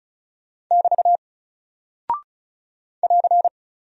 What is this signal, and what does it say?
Morse code, heard as a keyed tone, about 36 wpm, 700 Hz. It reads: <BT> <AR>
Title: Single Character (Letter:Number:Punctuation:Prosign) Alt 35wpm